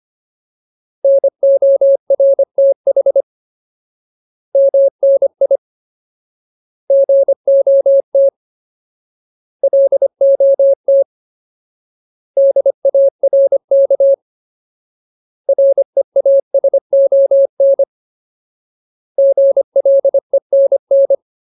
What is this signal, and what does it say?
Morse code, heard as a keyed tone, about 25 wpm, 550 Hz. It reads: NORTH MNI GOT LOT DARK REASON GLENN
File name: Ultimate Word List / 25WPM Ultimate